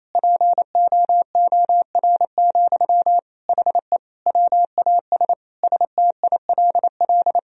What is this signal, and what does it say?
Morse code, heard as a keyed tone, about 27 wpm, 700 Hz. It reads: POOR, HE WAS STILL
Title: Sentences from Top 1000 Words - Review Part 3 - 28wpm